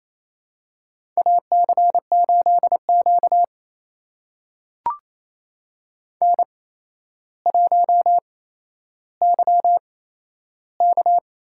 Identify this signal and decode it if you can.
Morse code, heard as a keyed tone, about 28 wpm, spaced as for 12 wpm, 700 Hz. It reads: AC8Q N1YK
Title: USA Call Signs - 8x Character Spacing - 28wpm